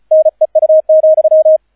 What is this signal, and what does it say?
Morse code, heard as a keyed tone, about 33 wpm, 630 Hz, with radio noise. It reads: NEU,